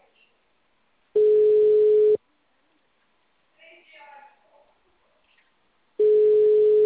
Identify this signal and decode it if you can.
Ringback tone (Europe, 425 Hz), with speech.